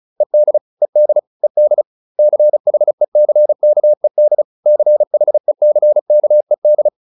Morse code, heard as a keyed tone, about 34 wpm, 600 Hz. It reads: ED ED ED CHECKED CHECKED